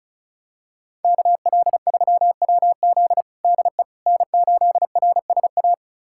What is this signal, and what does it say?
Morse code, heard as a keyed tone, about 36 wpm, 700 Hz. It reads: KL3WZ DE N8RSA
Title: USA Callsign DE Callsign - 5 Characters - 35wpm